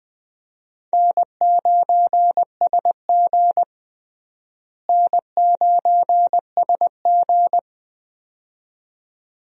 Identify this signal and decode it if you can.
Morse code, heard as a keyed tone, about 20 wpm, 700 Hz. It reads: N9SG N9SG
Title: USA Call Signs - 4 Characters - Repeated 2x - 20wpm